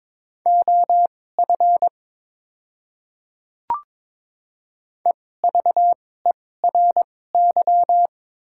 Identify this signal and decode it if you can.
Morse code, heard as a keyed tone, about 22 wpm, spaced as for 16 wpm, 700 Hz. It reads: OF EVERY
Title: 2x Character Spacing - Top 200 Words - 22wpm